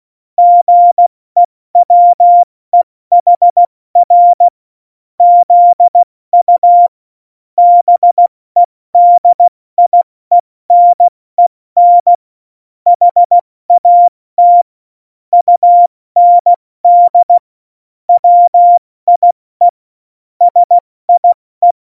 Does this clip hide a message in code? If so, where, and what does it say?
Yes: Morse code from 0.4 to 21.7 s, heard as a keyed tone, about 16 wpm, spaced as for 14 wpm, 700 Hz. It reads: GEWEHR ZU BEDIENEN HAT UND WIE SIE